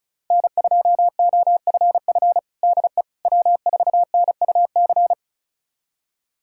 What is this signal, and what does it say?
Morse code, heard as a keyed tone, about 34 wpm, 700 Hz. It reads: N2OFF DE W4NUC
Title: USA Callsign DE Callsign - 5 Characters - 35wpm